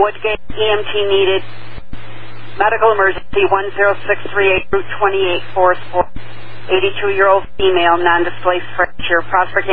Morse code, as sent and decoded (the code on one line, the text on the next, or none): none